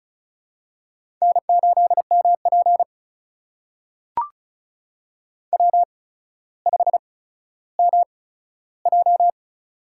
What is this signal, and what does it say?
Morse code, heard as a keyed tone, about 36 wpm, spaced as for 15 wpm, 700 Hz. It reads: N8MP W5MJ
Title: USA Call Signs - 4 Characters - 8x Character Spacing - 35wpm